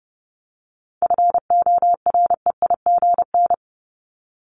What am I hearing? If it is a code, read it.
Morse code, heard as a keyed tone, about 30 wpm, 700 Hz. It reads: FOREIGN
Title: Top 1251-1500 Words 30wpm